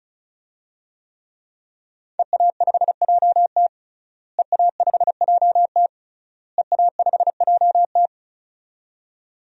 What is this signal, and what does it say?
Morse code, heard as a keyed tone, about 34 wpm, 700 Hz. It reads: EA5JT EA5JT EA5JT
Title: DXCC-100 Call Signs - 5 Characters - Repeated 3x - 35wpm